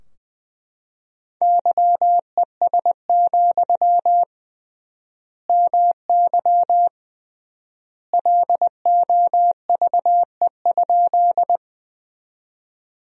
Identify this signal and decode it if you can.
Morse code, heard as a keyed tone, about 21 wpm, 700 Hz. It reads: YES, MY LOVE?
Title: 3x Word-Spacing - Sentences from Top 500 Words - 20wpm